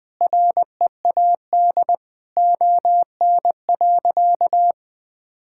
Morse code, heard as a keyed tone, about 20 wpm, 700 Hz. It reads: READ ON.
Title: Sentences with 9-Word Phrases - Spoken Beginning - 20wpm